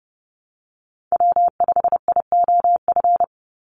Morse code, heard as a keyed tone, about 30 wpm, 700 Hz. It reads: W5IOF
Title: USA Call Signs 30wpm